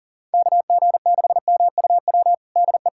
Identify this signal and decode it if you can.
Morse code, heard as a keyed tone, about 40 wpm, 700 Hz. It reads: KG6MUW DE
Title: USA Callsign DE Callsign 40wpm